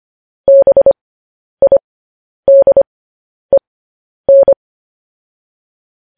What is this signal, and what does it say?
Morse code, heard as a keyed tone, about 25 wpm, spaced as for 10 wpm, 550 Hz. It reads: BIDEN
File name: podcast / News headlines-1x-25x10